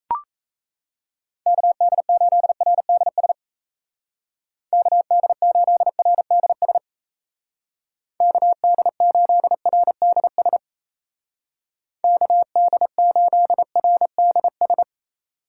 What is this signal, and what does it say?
Morse code, heard as a keyed tone, about 33 wpm, 700 Hz. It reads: KD8RDS KD8RDS KD8RDS KD8RDS
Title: Speed Racing - USA 2x3 Call Signs 28wpm